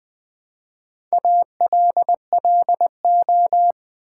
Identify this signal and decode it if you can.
Morse code, heard as a keyed tone, about 20 wpm, 700 Hz. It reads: ALLO
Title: Build A Word - 1 Letter At A Time - 5-Letter Words - 20wpm